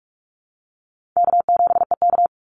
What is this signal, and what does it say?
Morse code, heard as a keyed tone, about 44 wpm, 700 Hz. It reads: K7EK
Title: Easy Call Signs Repeated 2x - 45wpm